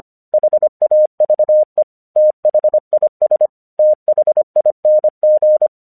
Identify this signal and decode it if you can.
Morse code, heard as a keyed tone, about 25 wpm, 610 Hz. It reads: HAVE THIS THING
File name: 1 English 25wpm / AQC24